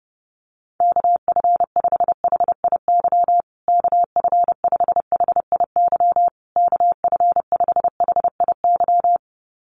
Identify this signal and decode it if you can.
Morse code, heard as a keyed tone, about 30 wpm, 700 Hz. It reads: KF5HIY KF5HIY KF5HIY
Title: USA Call Signs Repeated 3x 30wpm